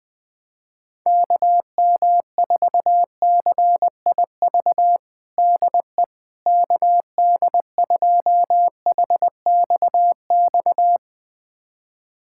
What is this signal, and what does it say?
Morse code, heard as a keyed tone, about 20 wpm, 700 Hz. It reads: KM4CIV DE KD2HXX
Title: USA Callsign DE Callsign 20wpm